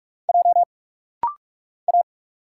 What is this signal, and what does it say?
Morse code, heard as a keyed tone, about 46 wpm, 700 Hz. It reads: J A
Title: Single Letter-Number Alt - Rapid-Fire 45wpm